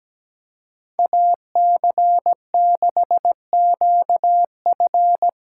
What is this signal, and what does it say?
Morse code, heard as a keyed tone, about 18 wpm, 700 Hz. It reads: AC6QF
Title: Speed Racing - USA 2x2 Call Signs 17wpm